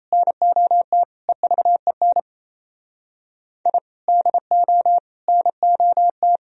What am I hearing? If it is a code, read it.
Morse code, heard as a keyed tone, about 35 wpm, 700 Hz. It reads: NOT EVEN I DO NOT
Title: Speed-Racing - Sets of 4 Words 28wpm